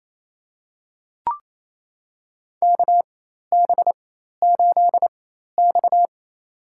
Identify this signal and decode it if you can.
Morse code, heard as a keyed tone, about 28 wpm, spaced as for 13 wpm, 700 Hz. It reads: KB8X
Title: USA Call Signs - 4 Characters - 4x Character Spacing - 28wpm